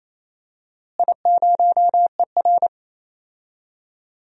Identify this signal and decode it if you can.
Morse code, heard as a keyed tone, about 28 wpm, 700 Hz. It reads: I0ER
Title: DXCC-100 Call Signs - 4 Characters - Repeated 2x - 28wpm